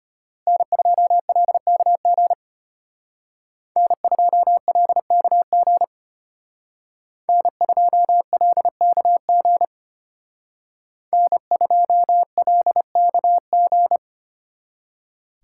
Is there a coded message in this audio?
Yes: Morse code from 0.5 to 14.0 s, heard as a keyed tone, about 32 wpm, 700 Hz. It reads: N2LKG N2LKG N2LKG N2LKG